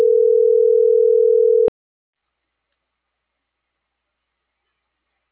Ringback tone (North America), 440 + 480 Hz.